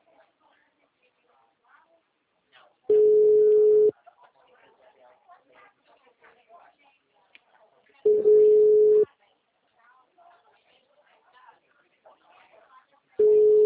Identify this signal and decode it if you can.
Ringback tone (Europe, 425 Hz), with speech.